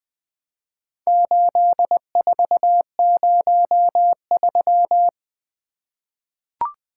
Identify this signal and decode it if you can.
Morse code, heard as a keyed tone, about 20 wpm, 700 Hz. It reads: 8403
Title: Four Letter-Number Combinations - 20wpm